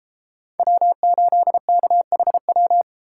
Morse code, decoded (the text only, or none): W8KHW